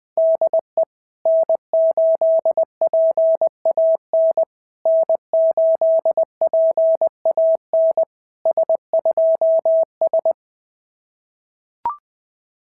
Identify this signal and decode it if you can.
Morse code, heard as a keyed tone, about 20 wpm, 660 Hz. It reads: DE N8PAN N8PAN S2S
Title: SOTA QSO Practice 20wpm